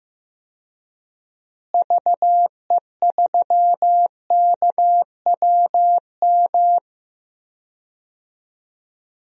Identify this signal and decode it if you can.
Morse code, heard as a keyed tone, about 15 wpm, 700 Hz. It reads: VE3KWM
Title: DXCC-100 Call Signs - 6 Characters - Repeated 3x - 15wpm